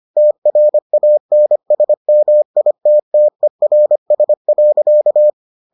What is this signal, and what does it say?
Morse code, heard as a keyed tone, about 25 wpm, 600 Hz. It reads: TRANSMITTERS.